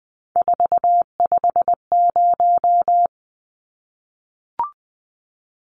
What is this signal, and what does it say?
Morse code, heard as a keyed tone, about 20 wpm, 700 Hz. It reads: 450
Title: Numbers - 3-Digit 20wpm